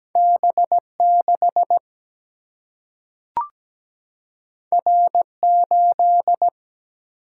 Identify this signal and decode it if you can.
Morse code, heard as a keyed tone, about 17 wpm, 700 Hz. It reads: B6 R8
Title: Single-Letter Single-Number 17wpm